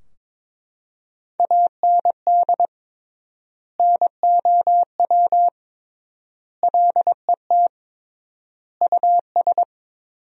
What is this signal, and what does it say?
Morse code, heard as a keyed tone, about 22 wpm, 700 Hz. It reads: AND NOW LET US
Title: 3x Word-Spacing - Sentences from Top 300 Words - 22wpm